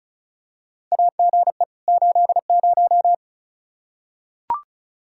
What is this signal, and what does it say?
Morse code, heard as a keyed tone, about 36 wpm, 700 Hz. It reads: AGE 80
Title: Practice QSO - Exchange 3 - More Info - 35wpm